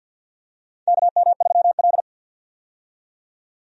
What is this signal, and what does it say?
Morse code, heard as a keyed tone, about 51 wpm, 700 Hz. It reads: KM3L
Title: USA Call Signs - 4 Characters - 50wpm